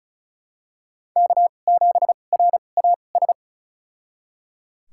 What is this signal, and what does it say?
Morse code, heard as a keyed tone, about 36 wpm, spaced as for 25 wpm, 700 Hz. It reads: K7RAS
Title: USA Call Signs - 5 Characters - 2x Character Spacing - 35wpm